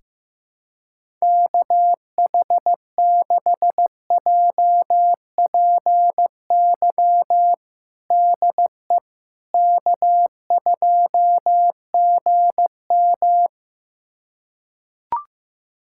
Morse code, heard as a keyed tone, about 16 wpm, 700 Hz. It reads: KH6JPY DE K2GM
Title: USA Callsign DE Callsign 15wpm